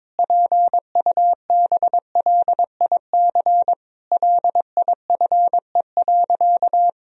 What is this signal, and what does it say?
Morse code, heard as a keyed tone, about 22 wpm, 700 Hz. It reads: PUBLIC LIFE.
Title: Sentences from Top 1000 Words - Review Part 3 - 22wpm